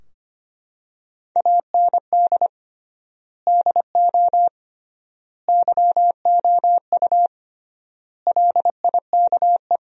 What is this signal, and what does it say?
Morse code, heard as a keyed tone, about 25 wpm, 700 Hz. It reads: AND DO YOU LIKE
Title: 3x Word-Spacing - Sentences from Top 100 Words - 25wpm